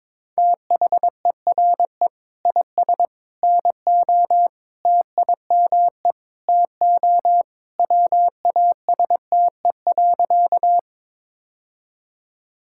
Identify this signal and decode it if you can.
Morse code, heard as a keyed tone, about 22 wpm, 700 Hz. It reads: THERE IS NO TIME TO WASTE.
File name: practice / 22wpm / Sentences with 3-Word Phrases - 22wpm